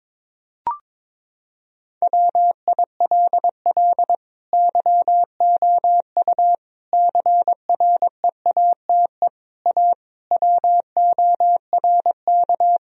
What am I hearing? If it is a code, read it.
Morse code, heard as a keyed tone, about 22 wpm, 700 Hz. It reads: WILL YOU CREATE A WORK
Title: Sentences from Top 1000 Words - Review Part 3 - 22wpm